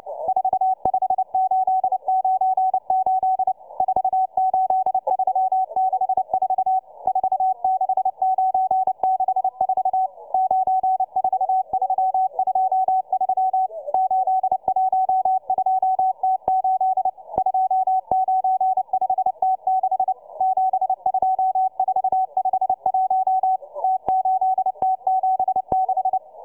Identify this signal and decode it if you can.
Morse code, heard as a keyed tone, about 29 wpm, 750 Hz, with radio noise. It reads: 45898 48364 46964 94323 812T8 295T6 72451 T8T76